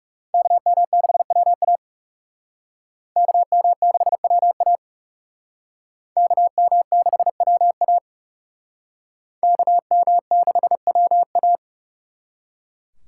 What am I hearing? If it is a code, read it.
Morse code, heard as a keyed tone, about 36 wpm, 700 Hz. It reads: KM6WA KM6WA KM6WA KM6WA
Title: Speed Racing - USA 2x2 Call Signs 30wpm